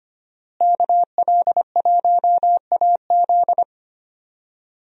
Morse code, heard as a keyed tone, about 25 wpm, 700 Hz. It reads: KL1AZ
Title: USA Call Signs - 5 Characters - Repeated 3x - 25wpm